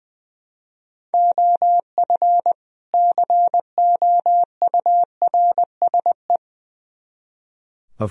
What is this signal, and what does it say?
Morse code, heard as a keyed tone, about 21 wpm, 700 Hz. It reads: OF COURSE
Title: Speed-Racing - Sets of 2 Words 20wpm